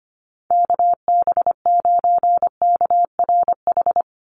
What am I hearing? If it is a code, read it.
Morse code, heard as a keyed tone, about 25 wpm, 700 Hz. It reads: KB9KRH
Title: USA Call Signs 25wpm